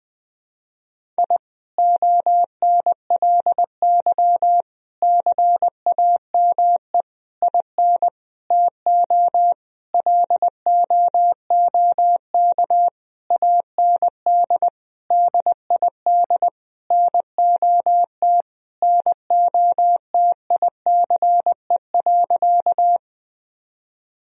Morse code, heard as a keyed tone, about 21 wpm, 700 Hz. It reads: I ONLY CAME IN TO LOOK AND DID NOT NOTICE.
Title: Sentences from Top 1000 Words - Review Part 2 - 20wpm